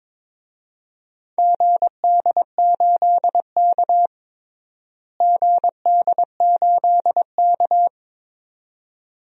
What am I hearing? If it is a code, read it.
Morse code, heard as a keyed tone, about 21 wpm, 700 Hz. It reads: GD8K GD8K
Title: DXCC-100 Call Signs - 4 Characters - Repeated 2x - 22wpm